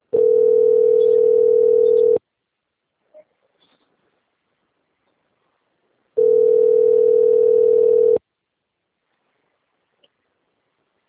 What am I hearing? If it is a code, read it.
Ringback tone (North America), 440 + 480 Hz.
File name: out- - - - - .6195